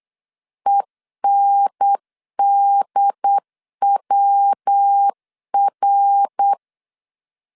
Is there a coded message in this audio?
Yes: Morse code from 0.7 to 6.5 s, heard as a keyed tone, about 8 wpm, 800 Hz. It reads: ENDWR